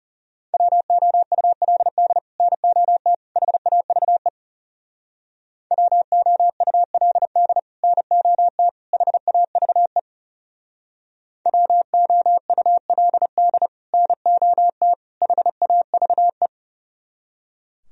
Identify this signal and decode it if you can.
Morse code, heard as a keyed tone, about 34 wpm, 700 Hz. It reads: WOULD NOT HAVE WOULD NOT HAVE WOULD NOT HAVE
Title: Speed-Racing - Sets of 3 Words 30wpm